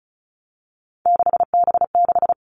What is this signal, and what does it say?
Morse code, heard as a keyed tone, about 34 wpm, 700 Hz. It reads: 6B6
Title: All Cut Numbers - 3-Digit 35wpm